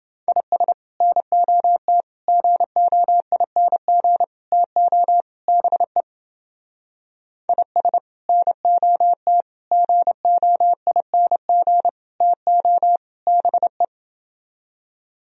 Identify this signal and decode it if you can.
Morse code, heard as a keyed tone, about 30 wpm, 700 Hz. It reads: IS NOT GOING TO BE IS NOT GOING TO BE